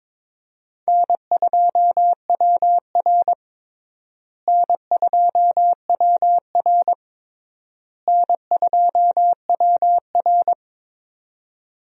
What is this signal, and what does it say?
Morse code, heard as a keyed tone, about 22 wpm, 700 Hz. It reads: N2WR N2WR N2WR
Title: USA Call Signs - 4 Characters - Repeated 3x - 22wpm